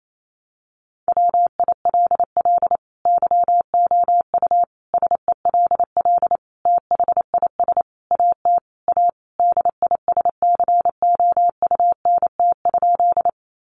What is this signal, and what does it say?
Morse code, heard as a keyed tone, about 28 wpm, 700 Hz. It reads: WILL YOU SELL THIS AT A DISCOUNT?
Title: Sentences from Top 1200 Words 28wpm